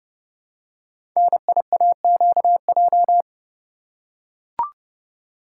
Morse code, heard as a keyed tone, about 30 wpm, 700 Hz. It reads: NIAQJ
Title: Code Groups - 5 Letters - 30wpm